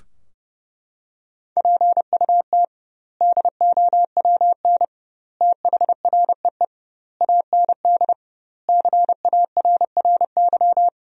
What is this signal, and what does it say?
Morse code, heard as a keyed tone, about 32 wpm, 700 Hz. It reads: PUT DOWN THREE AND CARRY
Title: 2x Word-Spacing - Sentences from Top 400 Words - 30wpm